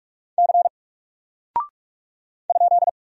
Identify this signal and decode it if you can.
Morse code, heard as a keyed tone, about 44 wpm, 700 Hz. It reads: C ?
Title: Single Character (Letter:Number:Punctuation:Prosign) 45wpm